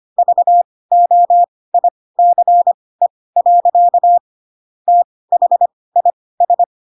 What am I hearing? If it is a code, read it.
Morse code, heard as a keyed tone, about 25 wpm, spaced as for 18 wpm, 700 Hz. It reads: VOICE. THIS
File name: Pinocchio / Pinocchio ch1 12 25 ik1hge